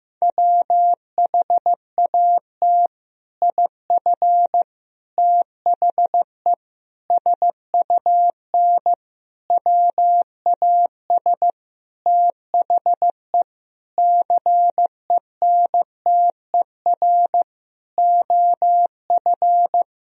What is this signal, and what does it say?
Morse code, heard as a keyed tone, about 16 wpm, 700 Hz. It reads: WHAT IF THE SUN WAS THE CENTER OF
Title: Sentences from Top 1000 Words - Review Part 2 - 15wpm